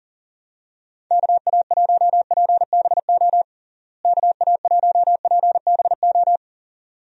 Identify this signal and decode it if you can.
Morse code, heard as a keyed tone, about 40 wpm, 700 Hz. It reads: KA1PBO KA1PBO
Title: USA Call Signs - 6 Characters - Repeated 2x - 40wpm